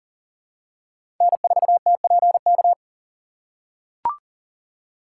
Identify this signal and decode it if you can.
Morse code, heard as a keyed tone, about 40 wpm, 700 Hz. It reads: N4TPK
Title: USA Call Signs - 5 Characters - 2x Character Spacing - 40wpm